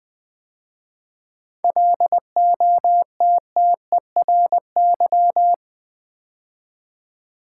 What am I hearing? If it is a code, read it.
Morse code, heard as a keyed tone, about 20 wpm, 700 Hz. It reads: LOTTERY
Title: 7 Letter Words - First 3 Letters Given - 20wpm